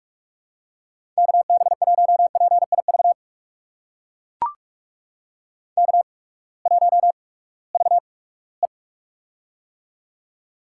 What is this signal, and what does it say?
Morse code, heard as a keyed tone, about 44 wpm, spaced as for 22 wpm, 700 Hz. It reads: KB1PIV K1VE
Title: USA Call Signs - 8x Character Spacing - 45wpm